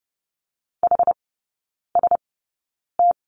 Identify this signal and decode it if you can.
Morse code, heard as a keyed tone, about 30 wpm, spaced as for 9 wpm, 700 Hz. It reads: HST
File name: Lesson 17e - Letter u - Three Characters 30wpm